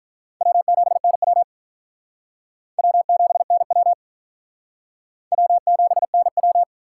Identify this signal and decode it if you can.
Morse code, heard as a keyed tone, about 42 wpm, 700 Hz. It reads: W7NW W7NW W7NW
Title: Speed Racing - USA 1x2 Call Signs 35wpm